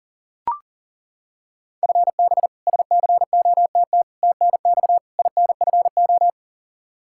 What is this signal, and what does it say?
Morse code, heard as a keyed tone, about 40 wpm, 700 Hz. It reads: FB SCOTT TNX INFO
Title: K1USN SST QSO Practice 40wpm